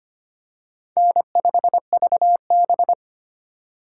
Morse code, heard as a keyed tone, about 25 wpm, 700 Hz. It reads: N5VB
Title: USA Call Signs - 4 Characters - 4x Character Spacing - 25wpm